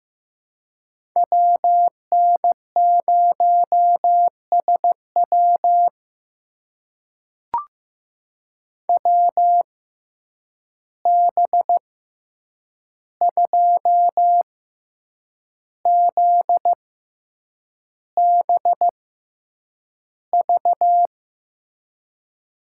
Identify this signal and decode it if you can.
Morse code, heard as a keyed tone, about 15 wpm, spaced as for 7 wpm, 700 Hz. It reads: WN0SW WB2ZBV
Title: USA Call Signs - 6x Character Spacing - 15wpm